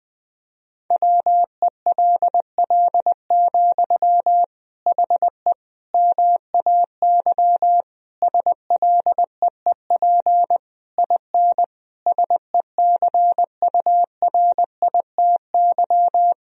Morse code, heard as a keyed tone, about 20 wpm, 700 Hz. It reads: WELL, HE MAY SLEEP IN SECURITY
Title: Sentences from Top 1000 Words - Review Part 4 - 20wpm